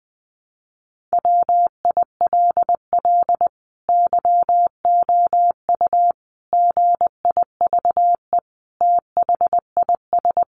Morse code, heard as a keyed tone, about 20 wpm, 700 Hz. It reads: WILL YOU GIVE THIS
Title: Sentences from Top 200 Words 20wpm